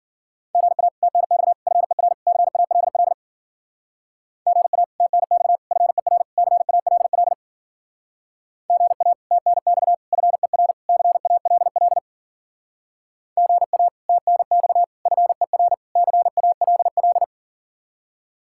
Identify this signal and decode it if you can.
Morse code, heard as a keyed tone, about 49 wpm, 700 Hz. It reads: GA TNX FER CALL GA TNX FER CALL GA TNX FER CALL GA TNX FER CALL
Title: Speed-Racing Common QSO Phrases - 40wpm